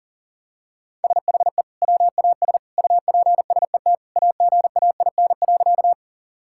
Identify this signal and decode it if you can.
Morse code, heard as a keyed tone, about 40 wpm, 700 Hz. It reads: SHE WAS UPSET AGAIN.
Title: Sentences from Top 1000 Words - Review Part 5 - 40wpm